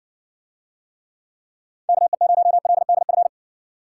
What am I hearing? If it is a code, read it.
Morse code, heard as a keyed tone, about 60 wpm, 700 Hz. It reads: KE0LDF